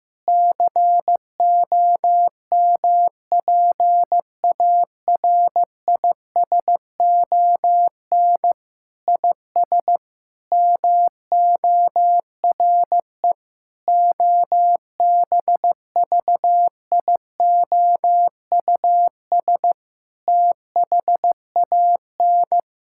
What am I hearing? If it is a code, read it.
Morse code, heard as a keyed tone, about 15 wpm, 700 Hz. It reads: COMPARISON IS MORE OBVIOUS THAN
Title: Sentences from Top 1600 Words 15wpm